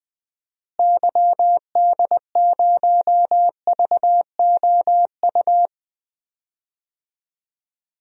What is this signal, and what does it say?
Morse code, heard as a keyed tone, about 20 wpm, 700 Hz. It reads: YD0VOU
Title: DXCC-100 Call Signs - 6 Characters - Repeated 3x - 20wpm